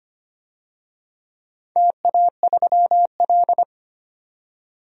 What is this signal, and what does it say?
Morse code, heard as a keyed tone, about 25 wpm, 700 Hz. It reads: TA3L
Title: DXCC-100 Call Signs - 4 Characters - 25wpm